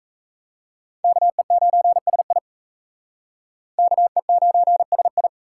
Morse code, heard as a keyed tone, about 34 wpm, 700 Hz. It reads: KE9SI KE9SI
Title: Speed Racing - USA 2x2 Call Signs 28wpm